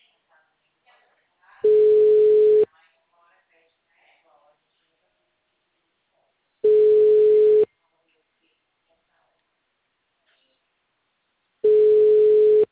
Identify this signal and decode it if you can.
Ringback tone (Europe, 425 Hz), with speech.
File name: out- -2056- - - .3127